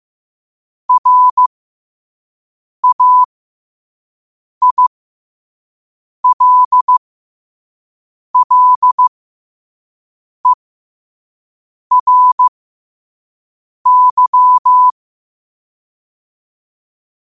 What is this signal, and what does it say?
Morse code, heard as a keyed tone, about 15 wpm, spaced as for 5 wpm, 1000 Hz. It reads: RAILLERY
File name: qotdfiles / 5wpm